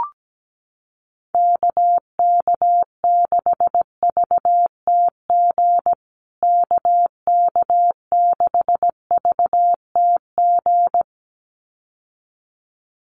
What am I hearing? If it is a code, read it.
Morse code, heard as a keyed tone, about 17 wpm, 700 Hz. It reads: KK6VTG KK6VTG
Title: USA Call Signs Repeated 2x 17wpm